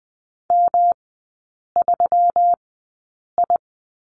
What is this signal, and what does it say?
Morse code, heard as a keyed tone, about 20 wpm, spaced as for 8 wpm, 700 Hz. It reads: M3I
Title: 121 Lesson 20e - Number 3 - Three Characters 20wpm